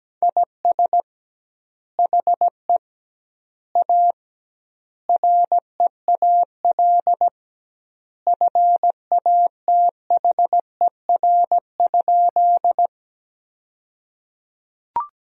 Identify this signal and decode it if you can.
Morse code, heard as a keyed tone, about 17 wpm, 700 Hz. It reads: IS HE A REAL FATHER?
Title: 2x Word-Spacing - Sentences from Top 300 Words - 17wpm